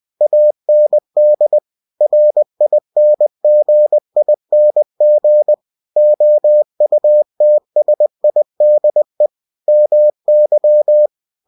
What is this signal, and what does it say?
Morse code, heard as a keyed tone, about 20 wpm, 600 Hz. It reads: AND RINGING OUTSIDE MY
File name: cw-books / winnie-the-pooh / winnie-20wpm-page-0052